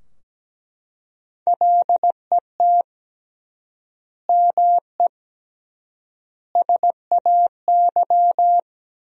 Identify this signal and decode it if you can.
Morse code, heard as a keyed tone, about 17 wpm, 700 Hz. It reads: LET ME SAY
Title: 3x Word-Spacing - Sentences from Top 300 Words - 17wpm